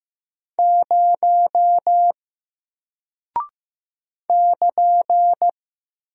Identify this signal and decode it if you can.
Morse code, heard as a keyed tone, about 15 wpm, 700 Hz. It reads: 0 <KN>
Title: Single Character (Letter:Number:Punctuation:Prosign) Alt 15wpm